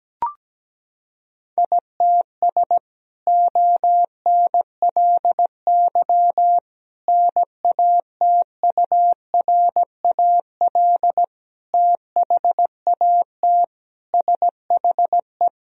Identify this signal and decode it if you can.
Morse code, heard as a keyed tone, about 17 wpm, 700 Hz. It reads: ITS ONLY NATURAL THAT SHE
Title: Sentences from Top 1000 Words - Review Part 4 - 17wpm